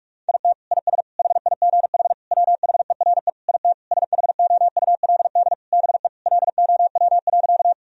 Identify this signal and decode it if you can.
Morse code, heard as a keyed tone, about 41 wpm, 700 Hz. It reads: IT IS HIGH WHERE IT SHOULD BE LOW.
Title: Sentences from Top 1000 Words - Review Part 1 - 45wpm